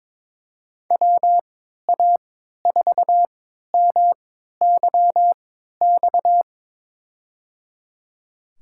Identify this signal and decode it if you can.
Morse code, heard as a keyed tone, about 21 wpm, spaced as for 12 wpm, 700 Hz. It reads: WA4MYX
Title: USA Call Signs - 6 Characters - 3x Character Spacing - 22wpm